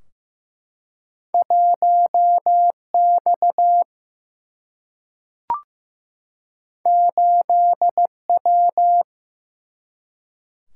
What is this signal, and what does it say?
Morse code, heard as a keyed tone, about 16 wpm, 700 Hz. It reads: 1 X 8 W
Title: Single-Letter Single-Number 15wpm